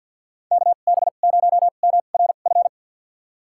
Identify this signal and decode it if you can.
Morse code, heard as a keyed tone, about 47 wpm, spaced as for 35 wpm, 700 Hz. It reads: KB0MRF
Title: USA Call Signs - 2x Character Spacing - 50wpm